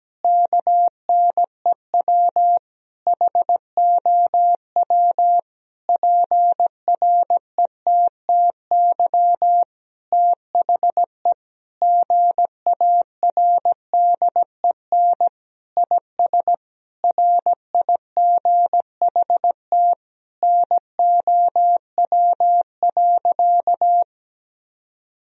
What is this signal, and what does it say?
Morse code, heard as a keyed tone, about 17 wpm, 700 Hz. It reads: KNEW HOW PRETTY THE GARDEN IS RIGHT NOW.
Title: Sentences from Top 1000 Words - Review Part 4 - 17wpm